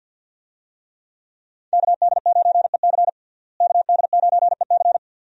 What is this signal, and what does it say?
Morse code, heard as a keyed tone, about 51 wpm, 700 Hz. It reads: KD9EC KD9EC
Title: USA Call Signs - 5 Characters - Repeated 2x - 50wpm